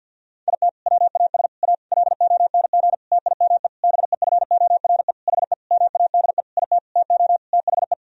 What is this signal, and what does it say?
Morse code, heard as a keyed tone, about 52 wpm, 700 Hz. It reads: IT WAS A LONG TIME BEFORE HE MADE IT TO THE
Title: Sentences from Top 1000 Words - Review Part 2 - 50wpm